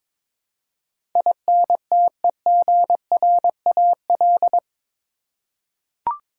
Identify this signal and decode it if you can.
Morse code, heard as a keyed tone, about 22 wpm, 700 Hz. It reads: INTEGRAL
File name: practice / 22wpm / Eight Letter Words - 22wpm